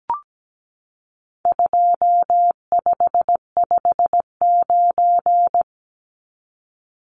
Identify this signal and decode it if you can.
Morse code, heard as a keyed tone, about 17 wpm, 700 Hz. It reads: 2559
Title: Numbers - 4-Digit 17wpm